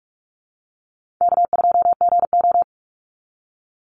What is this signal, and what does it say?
Morse code, heard as a keyed tone, about 44 wpm, 700 Hz. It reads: K2GO
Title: Call Signs Repeated 2x - 45wpm